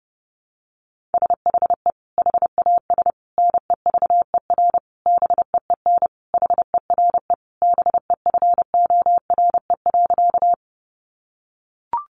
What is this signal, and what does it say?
Morse code, heard as a keyed tone, about 28 wpm, 700 Hz. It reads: SHE HAS NEVER BEEN HERE BEFORE.
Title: Additional Easy Phrases 30wpm